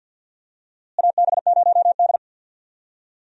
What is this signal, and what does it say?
Morse code, heard as a keyed tone, about 50 wpm, 700 Hz. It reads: AB0D
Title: USA Call Signs - 4 Characters - Repeated 2x - 50wpm